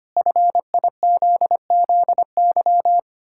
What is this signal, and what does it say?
Morse code, heard as a keyed tone, about 25 wpm, 700 Hz. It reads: FIZZY
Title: Pangram Challenge v2 - 25wpm